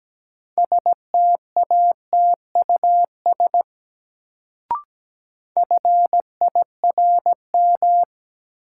Morse code, heard as a keyed tone, about 17 wpm, 700 Hz. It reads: STATUS FIRM
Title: Sentences from Top 1400 Words - New Word Drill Alt - 17wpm